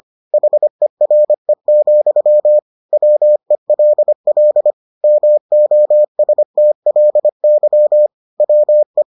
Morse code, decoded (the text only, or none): HERE, WELL MOSTLY WE